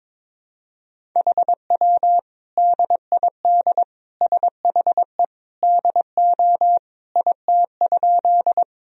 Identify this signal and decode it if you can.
Morse code, heard as a keyed tone, about 22 wpm, 700 Hz. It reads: HW DID SHE DO IT?
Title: Sentences from Top 100 Words and Abbreviations 22wpm